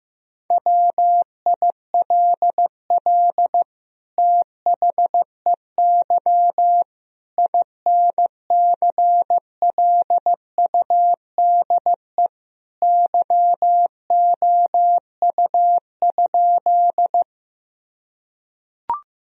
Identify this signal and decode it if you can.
Morse code, heard as a keyed tone, about 15 wpm, 700 Hz. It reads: WILL THEY INCLUDE YOU?
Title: Sentences from Top 1000 Words - Review Part 3 - 15wpm